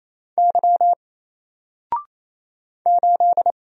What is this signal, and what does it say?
Morse code, heard as a keyed tone, about 27 wpm, 700 Hz. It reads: Y 8
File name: Single-Letter-Number-28wpm